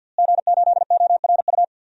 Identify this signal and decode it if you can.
Morse code, heard as a keyed tone, about 48 wpm, 700 Hz. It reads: G8ORU
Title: DXCC-100 Call Signs - 5 Characters - Repeated 2x - 50wpm